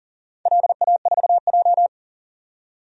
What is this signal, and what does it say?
Morse code, heard as a keyed tone, about 42 wpm, 700 Hz. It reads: LA4J
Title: DXCC-100 Call Signs - 4 Characters - 40wpm